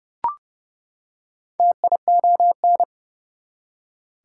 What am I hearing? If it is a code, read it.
Morse code, heard as a keyed tone, about 32 wpm, 700 Hz. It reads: TION
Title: Prefixes and Suffixes 30wpm